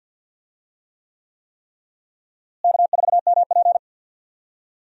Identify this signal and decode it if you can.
Morse code, heard as a keyed tone, about 52 wpm, 700 Hz. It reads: K4MP
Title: USA Call Signs - 4 Characters - Repeated 3x - 50wpm